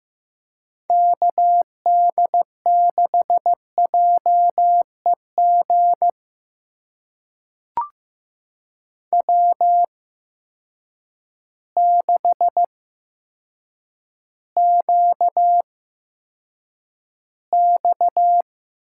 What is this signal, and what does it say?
Morse code, heard as a keyed tone, about 15 wpm, spaced as for 8 wpm, 700 Hz. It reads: KD6JEG W6QX
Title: USA Call Signs - 8x Character Spacing - 15wpm